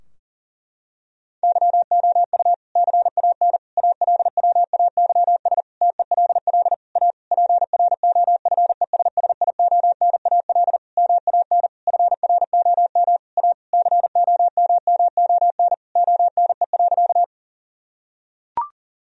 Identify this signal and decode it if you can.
Morse code, heard as a keyed tone, about 40 wpm, 700 Hz. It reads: YOU CAN ALWAYS TELL A PROFESSIONAL MAN FROM A COMMON ONE.
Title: Sentences from Top 1000 Words - Review Part 4 - 40wpm